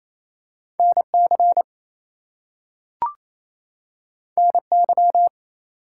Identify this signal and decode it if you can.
Morse code, heard as a keyed tone, about 28 wpm, 700 Hz. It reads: NC NY
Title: US States and Canadian Provinces 28wpm